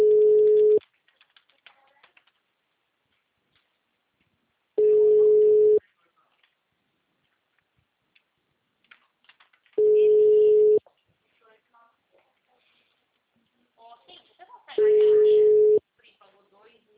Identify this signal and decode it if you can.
Ringback tone (Europe, 425 Hz), with speech.
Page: Campinas - SP